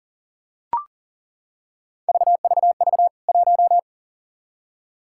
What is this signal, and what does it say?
Morse code, heard as a keyed tone, about 40 wpm, 700 Hz. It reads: VVV 1
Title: K1USN SST QSO Practice 40wpm